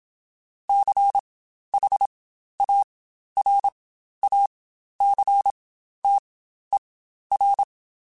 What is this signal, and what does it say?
Morse code, heard as a keyed tone, about 27 wpm, spaced as for 12 wpm, 800 Hz. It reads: CHARACTER
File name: Quotes Spaced / Quotes Spaced-mp3s / Character sp @25